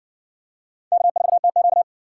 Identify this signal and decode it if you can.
Morse code, heard as a keyed tone, about 60 wpm, 700 Hz. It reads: K4TQ